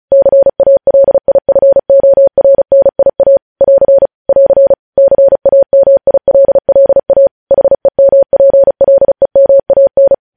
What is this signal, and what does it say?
Morse code, heard as a keyed tone, about 36 wpm, 550 Hz. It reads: CALIFORNIA <AR> <AR> CAMILLA HEMPLEMAN
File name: podcast / News headlines-1x-35x35